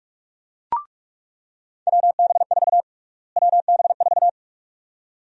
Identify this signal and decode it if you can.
Morse code, heard as a keyed tone, about 42 wpm, 700 Hz. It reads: WB4 WB4
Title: USA Call Sign Prefixes - Repeated 2x - 45wpm